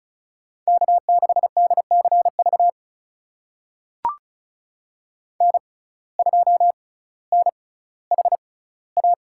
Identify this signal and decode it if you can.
Morse code, heard as a keyed tone, about 34 wpm, spaced as for 18 wpm, 700 Hz. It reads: K6DCV N2NHA
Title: USA Call Signs - 5 Characters - 6x Character Spacing - 35wpm